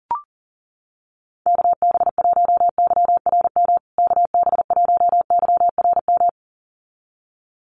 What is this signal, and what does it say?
Morse code, heard as a keyed tone, about 40 wpm, 700 Hz. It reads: KB1YRM KB1YRM
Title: USA Call Signs Repeated 2x 40wpm